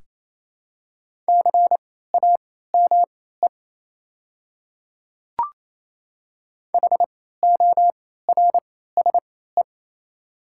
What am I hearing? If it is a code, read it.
Morse code, heard as a keyed tone, about 28 wpm, spaced as for 16 wpm, 700 Hz. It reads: CAME HORSE
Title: 3x Character Spacing - Top 300 Words - 28wpm